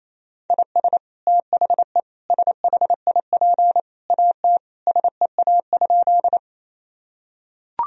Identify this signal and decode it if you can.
Morse code, heard as a keyed tone, about 28 wpm, 700 Hz. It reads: IS THE SHIP AT SEA?
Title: Sentences from Top 1000 Words - Review Part 2 - 28wpm